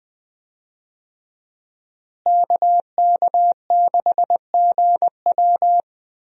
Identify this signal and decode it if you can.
Morse code, heard as a keyed tone, about 20 wpm, 700 Hz. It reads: KK6GW
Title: USA Call Signs - 5 Characters - Repeated 3x - 20wpm